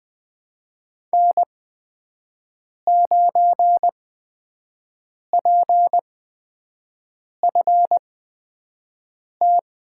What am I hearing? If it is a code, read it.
Morse code, heard as a keyed tone, about 21 wpm, spaced as for 5 wpm, 700 Hz. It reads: N9PFT
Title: USA Call Signs - 8x Character Spacing - 20wpm